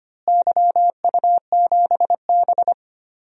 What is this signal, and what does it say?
Morse code, heard as a keyed tone, about 25 wpm, 700 Hz. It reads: YU7B
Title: DXCC-100 Call Signs - 4 Characters - Repeated 2x - 25wpm